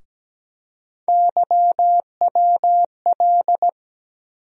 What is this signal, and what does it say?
Morse code, heard as a keyed tone, about 17 wpm, 700 Hz. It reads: YWL
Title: USA Call Sign Postfixes - 3 Letters 17wpm